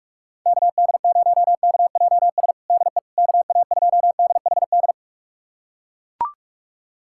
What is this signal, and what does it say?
Morse code, heard as a keyed tone, about 41 wpm, 700 Hz. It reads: KD0KJS DE KA2DHD
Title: USA Callsign DE Callsign - 6 Characters - 45wpm